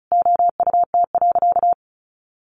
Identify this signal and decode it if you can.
Morse code, heard as a keyed tone, about 34 wpm, 700 Hz. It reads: OUT.
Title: Sentences from Top 1400 Words 35wpm